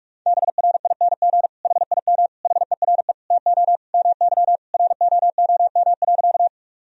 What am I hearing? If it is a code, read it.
Morse code, heard as a keyed tone, about 47 wpm, 700 Hz. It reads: BRING HIM HERE TO MY ROOM.
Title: Sentences from Top 1000 Words - Review Part 3 - 45wpm